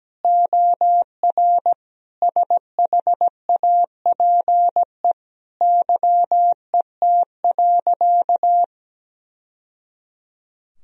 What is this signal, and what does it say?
Morse code, heard as a keyed tone, about 17 wpm, 700 Hz. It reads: OR SHAPE YET.
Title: Sentences from Top 1000 Words - Review Part 3 - 17wpm